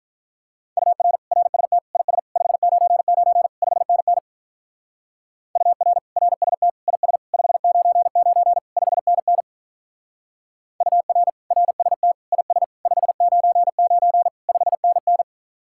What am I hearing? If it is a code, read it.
Morse code, heard as a keyed tone, about 46 wpm, 700 Hz. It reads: UR RST IS 599 5NN UR RST IS 599 5NN UR RST IS 599 5NN